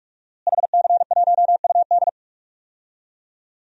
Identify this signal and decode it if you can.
Morse code, heard as a keyed tone, about 45 wpm, 700 Hz. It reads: HC1UD
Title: DXCC-100 Call Signs - 5 Characters - Repeated 3x - 45wpm